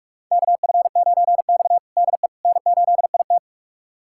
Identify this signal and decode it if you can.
Morse code, heard as a keyed tone, about 46 wpm, 700 Hz. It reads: KF9X DE N8IT
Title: USA Callsign DE Callsign - 4 Characters - 45wpm